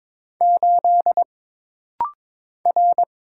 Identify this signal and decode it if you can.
Morse code, heard as a keyed tone, about 22 wpm, 700 Hz. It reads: 8 R
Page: This rapid-fire timing is geared to push your ICR (Instant Character Recognition) skills.